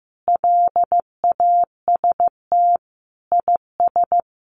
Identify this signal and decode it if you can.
Morse code, heard as a keyed tone, about 15 wpm, 700 Hz. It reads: LAST IS
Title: Sentences from Top 700 Words 15wpm